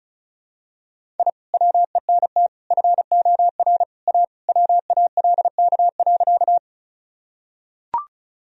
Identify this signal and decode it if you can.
Morse code, heard as a keyed tone, about 36 wpm, 700 Hz. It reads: I WENT FOR A WALK.
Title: Sentences from Top 1000 Words - Review Part 1 - 35wpm